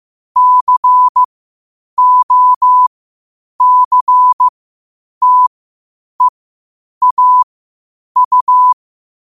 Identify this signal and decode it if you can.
Morse code, heard as a keyed tone, about 15 wpm, spaced as for 8 wpm, 1000 Hz. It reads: COCTEAU
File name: qotdfiles / 8wpm